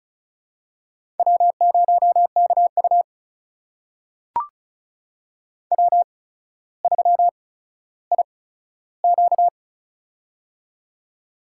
Morse code, heard as a keyed tone, about 36 wpm, spaced as for 15 wpm, 700 Hz. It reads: W0KU W3IQ
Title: USA Call Signs - 4 Characters - 8x Character Spacing - 35wpm